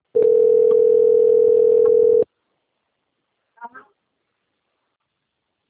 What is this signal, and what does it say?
Ringback tone (North America), 440 + 480 Hz, with speech.
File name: out- - - - - .56320